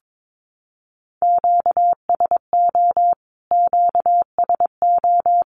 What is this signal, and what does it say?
Morse code, heard as a keyed tone, about 22 wpm, 700 Hz. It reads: QSO QSO
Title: Q-Codes Repeated 3x 22wpm